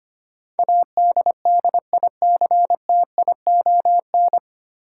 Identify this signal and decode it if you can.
Morse code, heard as a keyed tone, about 25 wpm, 700 Hz. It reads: ADDICTION
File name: Nine Letter Words - 25wpm